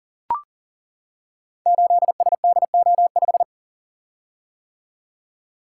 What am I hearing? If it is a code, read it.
Morse code, heard as a keyed tone, about 38 wpm, 690 Hz. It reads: 8SDO5
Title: Code Groups - 5 Letter:Numbers - 40wpm